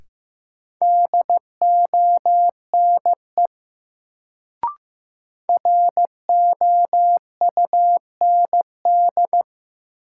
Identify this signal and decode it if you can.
Morse code, heard as a keyed tone, about 15 wpm, 700 Hz. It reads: DONE ROUND
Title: Sets of 5 Words Encore - New Word Drill - 15wpm